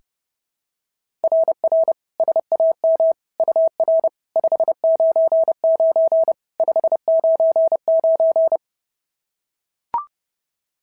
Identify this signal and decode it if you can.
Morse code, heard as a keyed tone, about 30 wpm, 690 Hz. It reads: RR SAM UR 599 599
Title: SOTA QSO Practice 30wpm